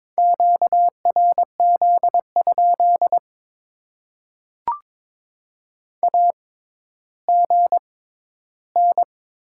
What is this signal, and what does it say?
Morse code, heard as a keyed tone, about 21 wpm, spaced as for 12 wpm, 700 Hz. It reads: QRZ? AGN
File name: Top 90 QSO Elements - 6x Character Spacing - 22wpm